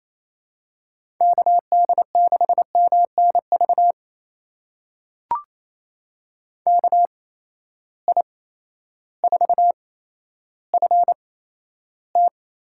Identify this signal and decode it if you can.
Morse code, heard as a keyed tone, about 28 wpm, spaced as for 13 wpm, 700 Hz. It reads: KD6MNV KI4FT
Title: USA Call Signs - 6 Characters - 8x Character Spacing - 28wpm